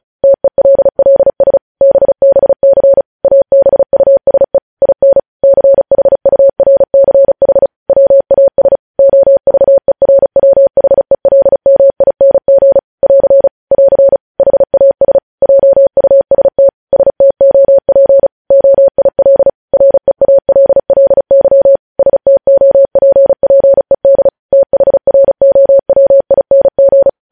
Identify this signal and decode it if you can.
Morse code, heard as a keyed tone, about 36 wpm, 550 Hz. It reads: TELLS BBC ABUSE IN CHURCH WAS OVERWHELMING <AR> <AR> HAS JUST STOP OIL REALLY STOPPED THROWING
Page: News headlines in Morse code at 35 wpm